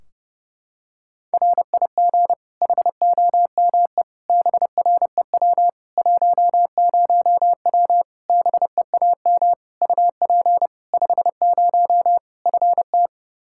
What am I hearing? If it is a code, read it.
Morse code, heard as a keyed tone, about 30 wpm, 700 Hz. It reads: RIG HOME BREW 10W BEAM UP 50 FT
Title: Practice QSO - Exchange 2 - Info - 30wpm